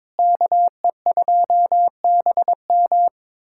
Morse code, heard as a keyed tone, about 22 wpm, 700 Hz. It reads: KE2BM
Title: USA Callsign DE Callsign - 5 Characters - 22wpm